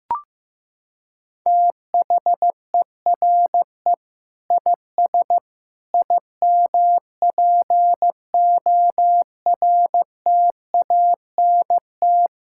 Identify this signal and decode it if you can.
Morse code, heard as a keyed tone, about 15 wpm, 700 Hz. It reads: THERE IS IMPORTANT
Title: Sentences from Top 1000 Words - Review Part 6 - 15wpm